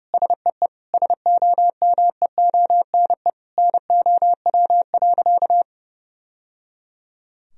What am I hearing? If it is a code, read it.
Morse code, heard as a keyed tone, about 30 wpm, 700 Hz. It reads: SEE SOMEONE NOW.
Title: Sentences from Top 1000 Words - Review Part 4 - 30wpm